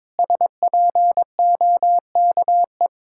Morse code, heard as a keyed tone, about 22 wpm, 700 Hz. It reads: SPOKE
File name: practice / 22wpm / Sentences with 8-Word Phrases - 22wpm